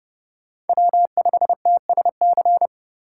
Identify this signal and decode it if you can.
Morse code, heard as a keyed tone, about 31 wpm, 700 Hz. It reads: W5TSC